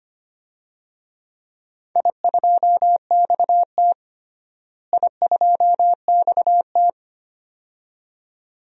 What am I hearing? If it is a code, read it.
Morse code, heard as a keyed tone, about 25 wpm, 700 Hz. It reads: I2XT I2XT
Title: DXCC-100 Call Signs - 4 Characters - Repeated 2x - 25wpm